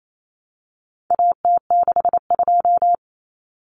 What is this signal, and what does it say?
Morse code, heard as a keyed tone, about 28 wpm, 700 Hz. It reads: AT62
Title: Common Cut Numbers - 4-Digit 28wpm